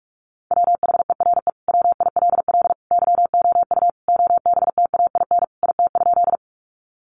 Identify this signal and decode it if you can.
Morse code, heard as a keyed tone, about 44 wpm, 700 Hz. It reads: WHERE WILL YOU OBTAIN IT?
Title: Sentences from Top 1300 Words 45wpm